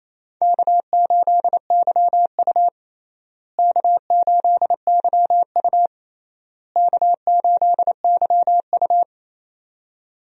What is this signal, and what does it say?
Morse code, heard as a keyed tone, about 28 wpm, 700 Hz. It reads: K8YU K8YU K8YU
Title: USA Call Signs - 4 Characters - Repeated 3x - 28wpm